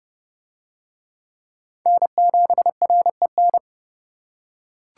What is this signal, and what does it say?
Morse code, heard as a keyed tone, about 29 wpm, 700 Hz. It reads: N7REN